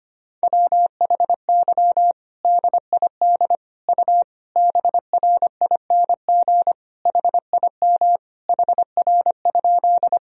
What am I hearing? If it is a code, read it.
Morse code, heard as a keyed tone, about 25 wpm, 700 Hz. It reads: WHY DID U BRING HIM HR?
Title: Sentences from Top 500 Words and Abbreviations 25wpm